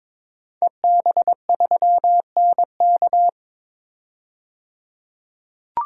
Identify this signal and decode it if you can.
Morse code, heard as a keyed tone, about 22 wpm, 700 Hz. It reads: EB3NK
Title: DXCC-100 Call Signs - 5 Characters - 22wpm